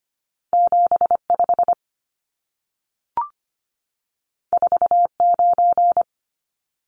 Morse code, heard as a keyed tone, about 25 wpm, 700 Hz. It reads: 75 49
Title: Numbers - 2-Digit 25wpm